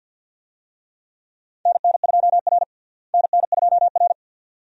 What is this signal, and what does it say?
Morse code, heard as a keyed tone, about 52 wpm, 700 Hz. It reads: NN2R NN2R
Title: USA Call Signs - 4 Characters - Repeated 2x - 50wpm